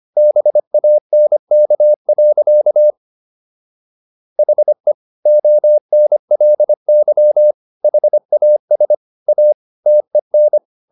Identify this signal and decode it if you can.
Morse code, heard as a keyed tone, about 25 wpm, 600 Hz. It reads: BANK. HE ONLY HAS A TEN